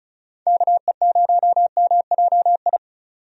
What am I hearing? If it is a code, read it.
Morse code, heard as a keyed tone, about 34 wpm, 700 Hz. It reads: KE0MJI
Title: USA Call Signs - 6 Characters - 35wpm